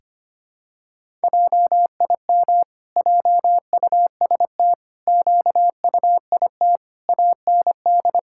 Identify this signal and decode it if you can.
Morse code, heard as a keyed tone, about 25 wpm, 700 Hz. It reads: JIM JUST QUIT AND
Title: Pangram Challenge v2 - 25wpm